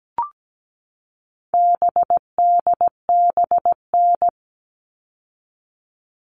Morse code, heard as a keyed tone, about 17 wpm, 700 Hz. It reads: BDBN
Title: All Cut Numbers - 4-Digit 17wpm